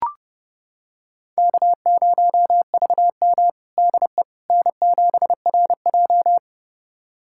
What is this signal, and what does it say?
Morse code, heard as a keyed tone, about 30 wpm, 700 Hz. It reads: K0VM DE N7RJ
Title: USA Callsign DE Callsign - 4 Characters - 30wpm